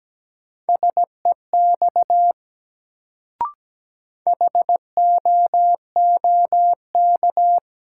Morse code, heard as a keyed tone, about 17 wpm, 700 Hz. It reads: SEX HOOK
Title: Sentences from Top 1600 Words - New Word Drill - 17wpm